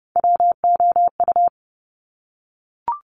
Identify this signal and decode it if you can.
Morse code, heard as a keyed tone, about 30 wpm, 690 Hz. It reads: WOU
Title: Top Three Letter Combinations in Words 30wpm